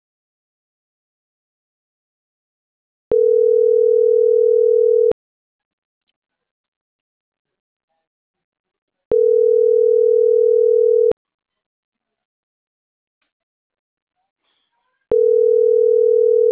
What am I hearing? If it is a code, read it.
Ringback tone (North America), 440 + 480 Hz.